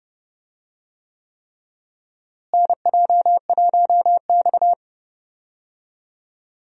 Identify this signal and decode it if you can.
Morse code, heard as a keyed tone, about 30 wpm, 700 Hz. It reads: NJ1X
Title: USA Call Signs - 4 Characters - Repeated 3x - 30wpm